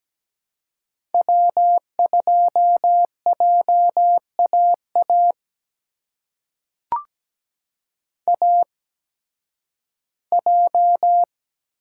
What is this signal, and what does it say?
Morse code, heard as a keyed tone, about 17 wpm, spaced as for 11 wpm, 700 Hz. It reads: W2JAA AJ
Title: USA Call Signs - 5 Characters - 8x Character Spacing - 17wpm